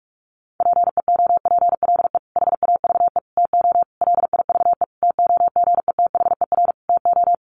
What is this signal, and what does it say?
Morse code, heard as a keyed tone, about 46 wpm, 700 Hz. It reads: PEOPLE HAVE TO LIVE TOGETHER TO
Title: Sentences from Top 1100 Words 45wpm